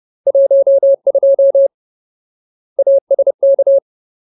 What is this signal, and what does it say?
Morse code, heard as a keyed tone, about 30 wpm, 550 Hz. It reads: 12 ASK
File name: Ultimate Word List / 30WPM Ultimate